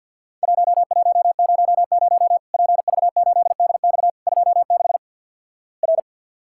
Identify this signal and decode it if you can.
Morse code, heard as a keyed tone, about 50 wpm, 700 Hz. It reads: 1100 PV8DX 26 R
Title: Worked All Europe DX Contest - QTC Exchange - 50wpm